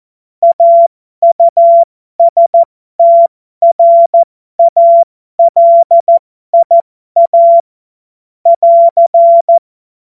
Morse code, heard as a keyed tone, about 14 wpm, spaced as for 12 wpm, 680 Hz. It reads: AUSTRALIA <AR>
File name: Morsekurs / Witze